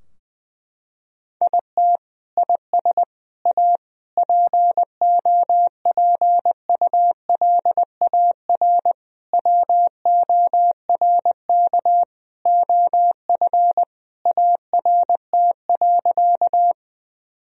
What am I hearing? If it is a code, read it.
Morse code, heard as a keyed tone, about 19 wpm, 700 Hz. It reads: IT IS A POPULAR WORK OF ART.
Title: Sentences from Top 1000 Words - Review Part 3 - 20wpm